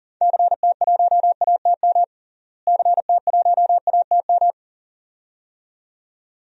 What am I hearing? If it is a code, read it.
Morse code, heard as a keyed tone, about 40 wpm, 700 Hz. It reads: CT1ATM CT1ATM
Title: DXCC-100 Call Signs - 6 Characters - Repeated 2x - 40wpm